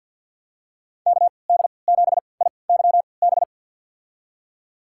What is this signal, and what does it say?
Morse code, heard as a keyed tone, about 45 wpm, spaced as for 27 wpm, 700 Hz. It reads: KD7IYB
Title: USA Call Signs - 3x Character Spacing - 50wpm